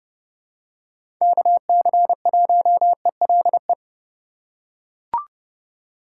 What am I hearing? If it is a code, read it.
Morse code, heard as a keyed tone, about 30 wpm, 700 Hz. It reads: KC1ELE
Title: USA Call Signs - 6 Characters - 3x Character Spacing - 30wpm